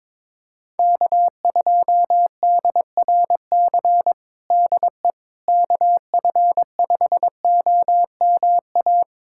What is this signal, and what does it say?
Morse code, heard as a keyed tone, about 22 wpm, 700 Hz. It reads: K2DRC DE KF5OMA
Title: USA Callsign DE Callsign 22wpm